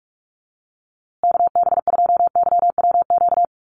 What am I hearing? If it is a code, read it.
Morse code, heard as a keyed tone, about 44 wpm, 700 Hz. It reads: KB2YWQ
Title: USA Call Signs Repeated 2x 45wpm